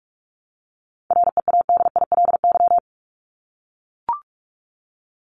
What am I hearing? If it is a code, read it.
Morse code, heard as a keyed tone, about 44 wpm, 700 Hz. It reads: READILY
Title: Top 1751-2000 Words 45wpm